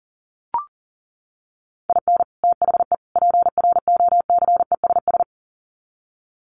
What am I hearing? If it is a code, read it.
Morse code, heard as a keyed tone, about 40 wpm, 700 Hz. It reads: IN THE PROCESS
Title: Sets of 3 Words 40wpm